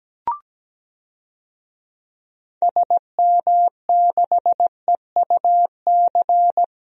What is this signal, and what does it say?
Morse code, heard as a keyed tone, about 17 wpm, 700 Hz. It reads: SM6EUC
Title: DXCC-100 Call Signs - 6 Characters - Repeated 2x - 17wpm